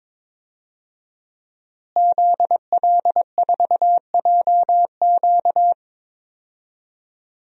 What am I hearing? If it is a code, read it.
Morse code, heard as a keyed tone, about 22 wpm, 700 Hz. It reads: ZL4JQ
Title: DXCC-100 Call Signs - 5 Characters - Repeated 3x - 22wpm